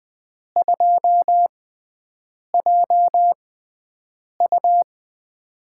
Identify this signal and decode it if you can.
Morse code, heard as a keyed tone, about 21 wpm, spaced as for 7 wpm, 700 Hz. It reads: 2JU
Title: USA Call Signs - 5 Characters - 6x Character Spacing - 20wpm